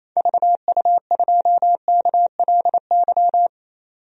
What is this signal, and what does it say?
Morse code, heard as a keyed tone, about 28 wpm, 700 Hz. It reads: VU2KLY
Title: DXCC-100 Call Signs - 6 Characters - 28wpm